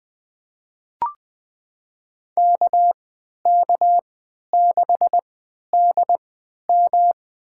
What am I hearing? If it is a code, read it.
Morse code, heard as a keyed tone, about 20 wpm, spaced as for 11 wpm, 700 Hz. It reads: KK6DM
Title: USA Call Signs - 5 Characters - 3x Character Spacing - 20wpm